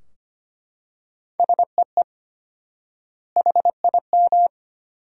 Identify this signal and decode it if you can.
Morse code, heard as a keyed tone, about 25 wpm, 700 Hz. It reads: SEE HIM
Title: 4x Word-Spacing - Sentences from Top 300 Words - 25wpm